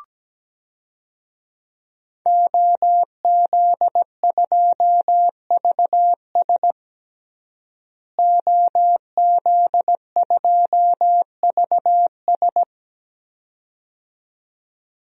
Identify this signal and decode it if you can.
Morse code, heard as a keyed tone, about 17 wpm, 700 Hz. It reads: OZ2VS OZ2VS
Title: DXCC-100 Call Signs - 5 Characters - Repeated 2x - 17wpm